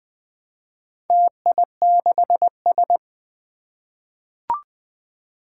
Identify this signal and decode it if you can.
Morse code, heard as a keyed tone, about 20 wpm, 700 Hz. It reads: TI6S
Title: Four Challenging Letter-Number Combinations 20wpm